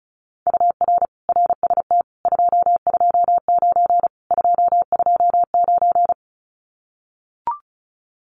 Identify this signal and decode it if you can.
Morse code, heard as a keyed tone, about 34 wpm, 700 Hz. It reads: UR RST 229 229
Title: Common QSO Phrases Encore 35wpm